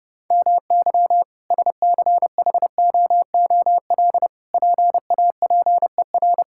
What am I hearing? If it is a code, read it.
Morse code, heard as a keyed tone, about 30 wpm, 700 Hz. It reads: MY SCHOOL PAPER
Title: Sentences from Top 1000 Words - Review Part 6 - 30wpm